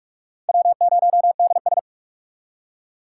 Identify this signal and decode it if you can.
Morse code, heard as a keyed tone, about 43 wpm, 700 Hz. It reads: W0DS
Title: USA Call Signs - 6x Character Spacing - 45wpm